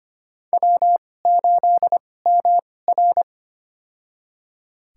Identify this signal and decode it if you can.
Morse code, heard as a keyed tone, about 25 wpm, spaced as for 18 wpm, 700 Hz. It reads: W8MR
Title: USA Call Signs - 4 Characters - 2x Character Spacing - 25wpm